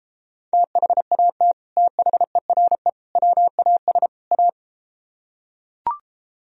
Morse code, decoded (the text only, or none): THAT THERE WAS A